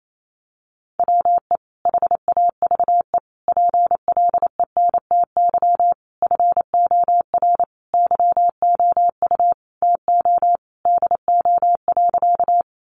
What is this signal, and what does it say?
Morse code, heard as a keyed tone, about 28 wpm, 700 Hz. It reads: WE HAVE PLENTY FOR YOU TO DO.
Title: Sentences from Top 1300 Words 28wpm